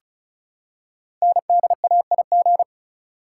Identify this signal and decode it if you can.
Morse code, heard as a keyed tone, about 36 wpm, 700 Hz. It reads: NDAIG
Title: Code Groups - 5 Letters - 35wpm